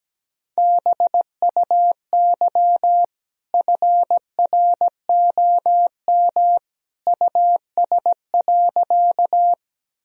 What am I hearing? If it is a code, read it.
Morse code, heard as a keyed tone, about 17 wpm, 700 Hz. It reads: BUY FROM US.
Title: Sentences from Top 1000 Words - Review Part 4 - 17wpm